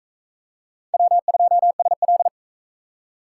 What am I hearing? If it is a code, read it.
Morse code, heard as a keyed tone, about 37 wpm, 700 Hz. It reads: W2SL